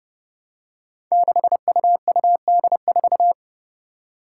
Morse code, heard as a keyed tone, about 28 wpm, 700 Hz. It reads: 6UUD4
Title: Five Challenging Letter-Number Combinations 30wpm